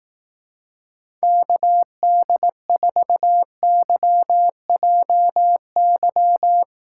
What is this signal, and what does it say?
Morse code, heard as a keyed tone, about 18 wpm, 700 Hz. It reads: KD4YJY
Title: Speed Racing - USA Call Signs 15wpm